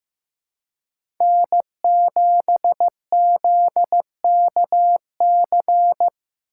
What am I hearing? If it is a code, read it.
Morse code, heard as a keyed tone, about 15 wpm, 700 Hz. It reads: N7ZKC
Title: USA Call Signs - 5 Characters - 3x Character Spacing - 15wpm